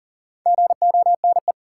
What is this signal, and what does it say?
Morse code, heard as a keyed tone, about 39 wpm, 700 Hz. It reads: GONE
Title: Sentences from Top 1200 Words - New Word Drill Alt - 40wpm